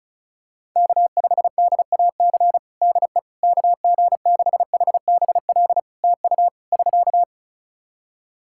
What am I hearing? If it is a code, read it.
Morse code, heard as a keyed tone, about 35 wpm, 700 Hz. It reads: K5DAC DE KG6HBL TU <SK>
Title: K1USN SST QSO Practice 35wpm